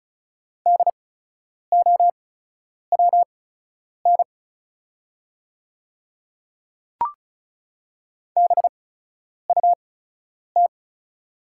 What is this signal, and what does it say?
Morse code, heard as a keyed tone, about 36 wpm, spaced as for 10 wpm, 700 Hz. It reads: DOWN BUT
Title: 8x Character Spacing - Top 100 Words - 35wpm